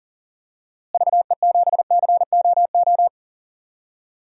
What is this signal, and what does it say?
Morse code, heard as a keyed tone, about 40 wpm, 700 Hz. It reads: VE7COO
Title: DXCC-100 Call Signs - 6 Characters - Repeated 3x - 40wpm